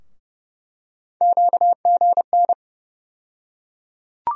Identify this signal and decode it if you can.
Morse code, heard as a keyed tone, about 30 wpm, 700 Hz. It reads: QGN
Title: USA Call Sign Postfixes - 3 Letters 30wpm